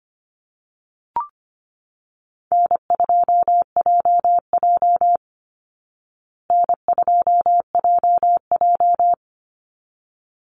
Morse code, heard as a keyed tone, about 25 wpm, 700 Hz. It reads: N2JJ N2JJ
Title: Call Signs Repeated 3x - 25wpm